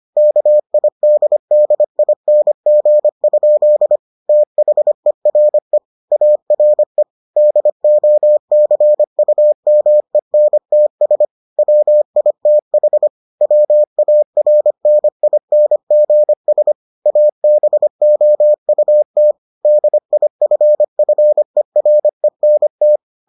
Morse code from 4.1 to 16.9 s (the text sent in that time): THERE ARE DOCUMENTS WITH WARNINGS